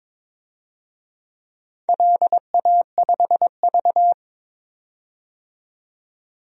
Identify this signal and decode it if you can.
Morse code, heard as a keyed tone, about 22 wpm, 700 Hz. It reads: LA5V
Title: DXCC-100 Call Signs - 4 Characters - 22wpm